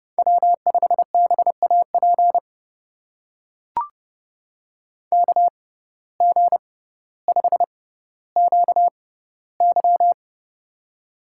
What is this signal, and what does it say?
Morse code, heard as a keyed tone, about 29 wpm, spaced as for 15 wpm, 700 Hz. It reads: W5BAP KG5QY
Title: USA Call Signs - 6x Character Spacing - 30wpm